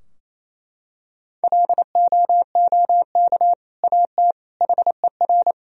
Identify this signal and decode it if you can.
Morse code, heard as a keyed tone, about 28 wpm, 700 Hz. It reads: LOOK AT HER
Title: Sentences from Top 1000 Words - Review Part 2 - 28wpm